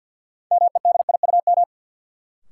Morse code, heard as a keyed tone, about 49 wpm, 700 Hz. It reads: MEDIUM
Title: Sentences from Top 900 Words - New Word Drill - 50wpm